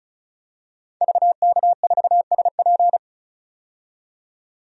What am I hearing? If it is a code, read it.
Morse code, heard as a keyed tone, about 35 wpm, 700 Hz. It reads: VK4SP
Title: DXCC-100 Call Signs - 5 Characters - 35wpm